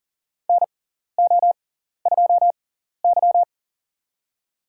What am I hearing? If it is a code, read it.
Morse code, heard as a keyed tone, about 40 wpm, spaced as for 14 wpm, 700 Hz. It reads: NO2Y
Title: USA Call Signs - 6x Character Spacing - 40wpm